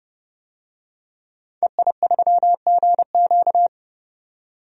Morse code, heard as a keyed tone, about 30 wpm, 690 Hz. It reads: EI3GQ
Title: DXCC-100 Call Signs - 5 Characters - 30wpm